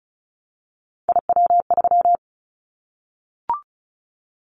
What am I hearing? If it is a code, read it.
Morse code, heard as a keyed tone, about 34 wpm, 700 Hz. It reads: IW3
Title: Callsign Prefixes for Top 25 DXCC - Repeated 3x - 35wpm